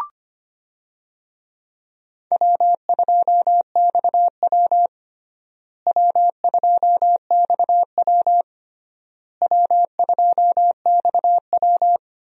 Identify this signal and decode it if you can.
Morse code, heard as a keyed tone, about 25 wpm, 700 Hz. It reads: W2XW W2XW W2XW
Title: USA Call Signs - 4 Characters - Repeated 3x - 25wpm